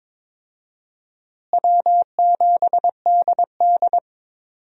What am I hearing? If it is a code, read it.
Morse code, heard as a keyed tone, about 22 wpm, 700 Hz. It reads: W7DD
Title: USA Call Signs - 4 Characters - Repeated 3x - 22wpm